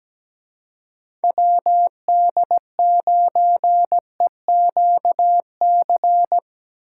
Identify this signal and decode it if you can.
Morse code, heard as a keyed tone, about 17 wpm, 700 Hz. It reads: WD9EQC
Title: USA Call Signs - 6 Characters - 4x Character Spacing - 17wpm